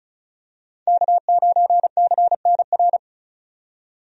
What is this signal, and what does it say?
Morse code, heard as a keyed tone, about 36 wpm, 700 Hz. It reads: K9CNR
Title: USA Call Signs - 5 Characters - 35wpm